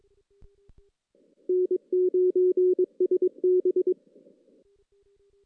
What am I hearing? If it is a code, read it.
Morse code, heard as a keyed tone, about 22 wpm, 370 Hz. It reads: N9SB
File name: 2016 JAN CW NAQP / 2016 Jan naqp k5ot - 0-5